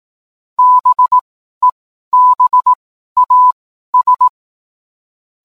Morse code, heard as a keyed tone, about 18 wpm, spaced as for 12 wpm, 1000 Hz. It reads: BEBAS